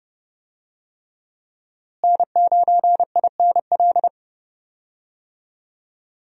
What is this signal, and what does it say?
Morse code, heard as a keyed tone, about 30 wpm, 700 Hz. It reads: N9INL
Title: USA Call Signs - 5 Characters - Repeated 3x - 30wpm